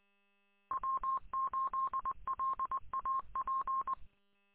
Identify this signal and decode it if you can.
Morse code, heard as a keyed tone, about 23 wpm, 1060 Hz. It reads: W8LAP
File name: w8lap- -combined